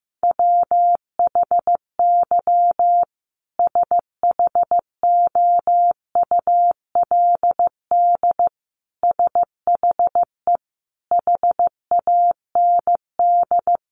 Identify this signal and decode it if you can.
Morse code, heard as a keyed tone, about 15 wpm, 700 Hz. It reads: WHY SHOULD SHE HAND
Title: Sentences from Top 900 Words 15wpm